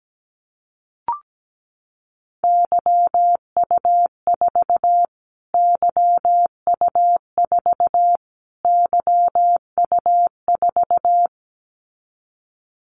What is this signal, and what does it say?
Morse code, heard as a keyed tone, about 17 wpm, 700 Hz. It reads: YU4 YU4 YU4
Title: Callsign Prefixes for Top 25 DXCC - Repeated 3x - 17wpm